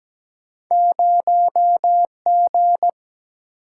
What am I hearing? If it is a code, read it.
Morse code, heard as a keyed tone, about 17 wpm, 700 Hz. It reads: 0 G
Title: Single-Letter Single-Number 17wpm